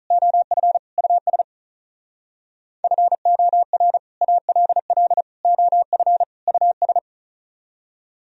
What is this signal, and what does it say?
Morse code, heard as a keyed tone, about 46 wpm, 700 Hz. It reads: OF US FOR ALL OF US